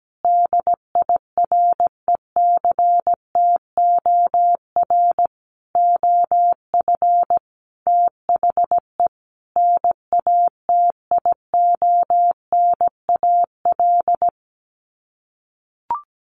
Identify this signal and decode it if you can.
Morse code, heard as a keyed tone, about 17 wpm, 700 Hz. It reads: DIRECTOR OF THE NATIONAL
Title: Sets of 5 Words Encore2 17wpm